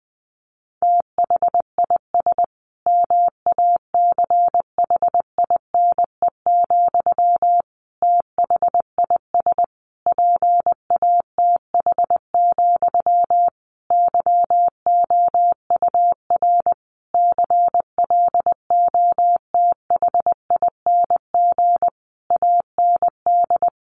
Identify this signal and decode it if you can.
Morse code, heard as a keyed tone, about 20 wpm, 700 Hz. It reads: THIS MACHINE, THIS PATH, YOUR CLOTHING AND
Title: A Sound Of Thunder 20wpm